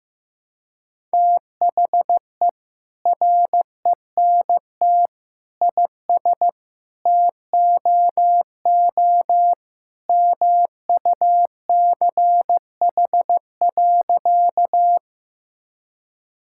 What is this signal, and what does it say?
Morse code, heard as a keyed tone, about 15 wpm, 700 Hz. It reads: THE RENT IS TOO MUCH.
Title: Sentences from Top 1000 Words - Review Part 5 - 15wpm